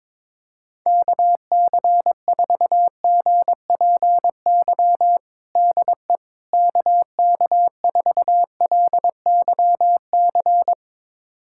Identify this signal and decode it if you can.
Morse code, heard as a keyed tone, about 21 wpm, 700 Hz. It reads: KC4GPY DE KK4LYC
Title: USA Callsign DE Callsign - 6 Characters - 22wpm